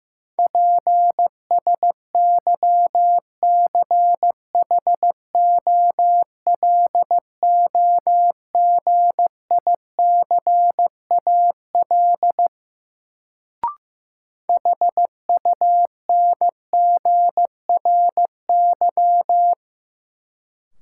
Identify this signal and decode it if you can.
Morse code, heard as a keyed tone, about 15 wpm, 700 Hz. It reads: PSYCHOLOGICAL HUNGRY
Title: Sentences from Top 1400 Words - New Word Drill - 15wpm